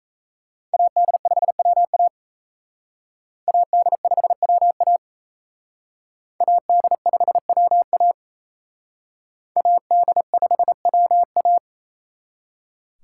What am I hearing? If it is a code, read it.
Morse code, heard as a keyed tone, about 36 wpm, 700 Hz. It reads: AD5WA AD5WA AD5WA AD5WA